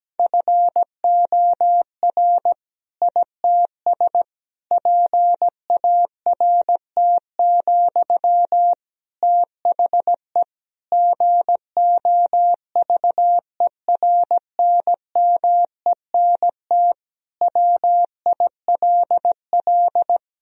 Morse code, heard as a keyed tone, about 17 wpm, 700 Hz. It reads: FOR ITS PART, THE GOVERNMENT WILL
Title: Sentences from Top 1000 Words - Review Part 3 - 17wpm